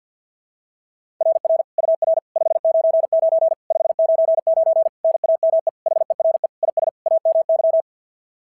Morse code, heard as a keyed tone, about 50 wpm, 660 Hz. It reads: RR UR 599 599 NAME HERE IS AMY
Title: SOTA QSO Practice 50wpm